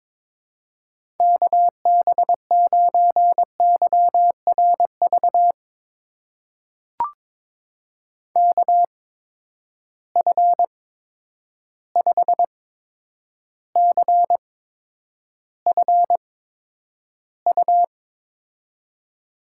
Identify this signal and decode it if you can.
Morse code, heard as a keyed tone, about 22 wpm, spaced as for 9 wpm, 700 Hz. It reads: KB9YRV KF5CFU
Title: USA Call Signs - 6 Characters - 8x Character Spacing - 22wpm